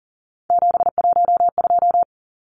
Morse code, heard as a keyed tone, about 40 wpm, 700 Hz. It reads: 712
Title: Numbers - 3-Digit 40wpm